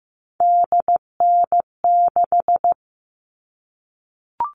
Morse code, heard as a keyed tone, about 15 wpm, 700 Hz. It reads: DN6
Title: All Cut Numbers - 3-Digit 15wpm